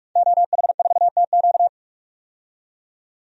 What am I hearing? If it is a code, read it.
Morse code, heard as a keyed tone, about 46 wpm, 700 Hz. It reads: OH4TQ
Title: DXCC-100 Call Signs - 5 Characters - 45wpm